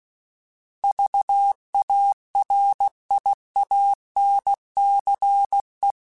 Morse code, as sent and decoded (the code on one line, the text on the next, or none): ...- .- .-. .. .- -. -.-. .
VARIANCE